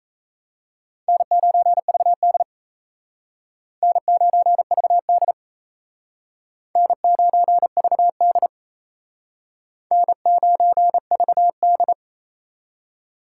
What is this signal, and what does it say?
Morse code, heard as a keyed tone, about 35 wpm, 700 Hz. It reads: N9VD N9VD N9VD N9VD
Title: Speed Racing - USA Call Signs 28wpm